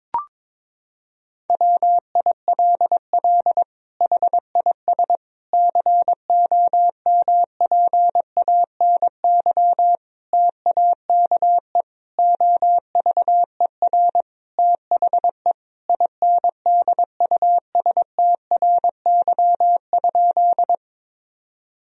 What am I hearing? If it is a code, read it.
Morse code, heard as a keyed tone, about 22 wpm, 700 Hz. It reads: WILL HIS COMPANY TAKE OVER THE INDUSTRY?
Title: Sentences from Top 1000 Words - Review Part 4 - 22wpm